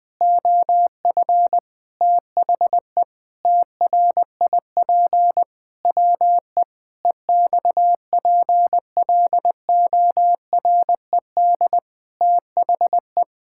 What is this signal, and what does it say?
Morse code, heard as a keyed tone, about 20 wpm, 700 Hz. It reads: OF THE TRIP WE EXPLORED THE
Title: Sentences with 6-Word Phrases - 20wpm